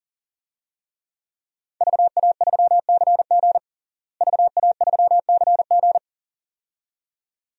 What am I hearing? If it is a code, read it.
Morse code, heard as a keyed tone, about 40 wpm, 700 Hz. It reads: VA3CG VA3CG
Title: DXCC-100 Call Signs - 5 Characters - Repeated 2x - 40wpm